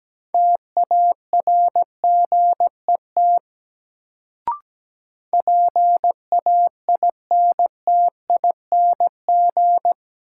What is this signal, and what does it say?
Morse code, heard as a keyed tone, about 17 wpm, 700 Hz. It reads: TARGET PAINTING
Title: Sentences from Top 1200 Words - New Word Drill Alt - 17wpm